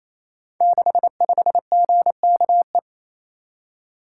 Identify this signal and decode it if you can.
Morse code, heard as a keyed tone, about 27 wpm, 700 Hz. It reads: 65GKE
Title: Code Groups - 5 Letter:Numbers - 28wpm